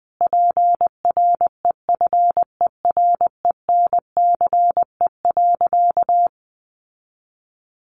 Morse code, heard as a keyed tone, about 20 wpm, 700 Hz. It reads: PREFERENCE.
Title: Sentences from Top 1500 Words 20wpm